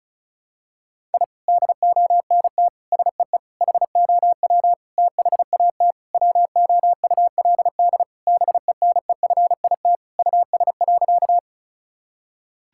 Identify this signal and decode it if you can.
Morse code, heard as a keyed tone, about 36 wpm, 700 Hz. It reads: I DONT SEE HOW THAT WOULD BENEFIT US.
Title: Sentences from Top 1000 Words - Review Part 6 - 35wpm